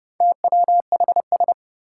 Morse code, heard as a keyed tone, about 30 wpm, 700 Hz. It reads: TWHS
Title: Four Challenging Letter-Number Combinations 30wpm